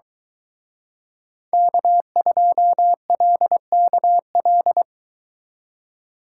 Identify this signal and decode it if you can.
Morse code, heard as a keyed tone, about 23 wpm, 700 Hz. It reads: K2LKL